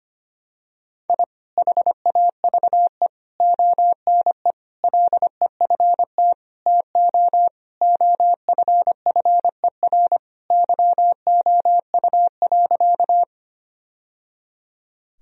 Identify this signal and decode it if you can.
Morse code, heard as a keyed tone, about 25 wpm, 700 Hz. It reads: I HAVE ONE LEFT TO OFFER YOU.
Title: Sentences from Top 1000 Words - Review Part 4 - 25wpm